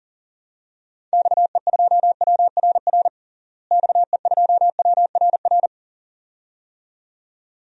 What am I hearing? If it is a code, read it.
Morse code, heard as a keyed tone, about 40 wpm, 700 Hz. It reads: XE2WRR XE2WRR
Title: DXCC-100 Call Signs - 6 Characters - Repeated 2x - 40wpm